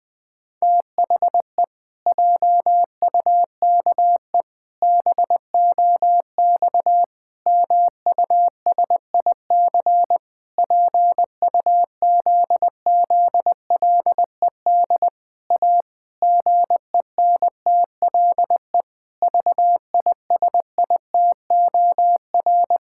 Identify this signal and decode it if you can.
Morse code, heard as a keyed tone, about 20 wpm, 700 Hz. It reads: THE JUKE BOX MUSIC PUZZLED A GENTLE VISITOR
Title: Pangram Challenge v2 - 20wpm